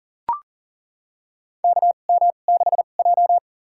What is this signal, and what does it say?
Morse code, heard as a keyed tone, about 40 wpm, spaced as for 29 wpm, 700 Hz. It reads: KM6J
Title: USA Call Signs - 4 Characters - 2x Character Spacing - 40wpm